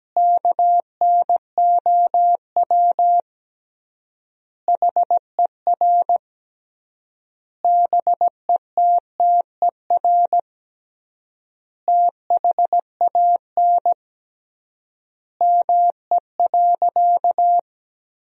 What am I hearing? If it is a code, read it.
Morse code, heard as a keyed tone, about 17 wpm, 700 Hz. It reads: KNOW HER BETTER THAN ME.
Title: 3x Word-Spacing - Sentences from Top 400 Words - 17wpm